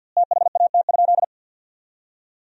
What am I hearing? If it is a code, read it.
Morse code, heard as a keyed tone, about 50 wpm, 700 Hz. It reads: THAT?
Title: 2x Word-Spacing - Sentences from Top 300 Words - 50wpm